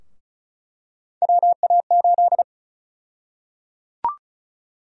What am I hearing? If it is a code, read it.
Morse code, heard as a keyed tone, about 36 wpm, 700 Hz. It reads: WA8
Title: USA Call Sign Prefixes - 35wpm